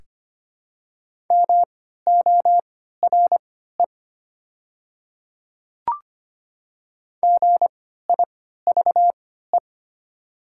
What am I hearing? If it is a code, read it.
Morse code, heard as a keyed tone, about 25 wpm, spaced as for 14 wpm, 700 Hz. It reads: MORE GIVE
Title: 3x Character Spacing - Top 200 Words - 25wpm